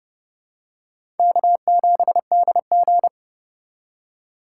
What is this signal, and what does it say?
Morse code, heard as a keyed tone, about 30 wpm, 700 Hz. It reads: K7DG
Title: USA Call Signs - 4 Characters - 4x Character Spacing - 30wpm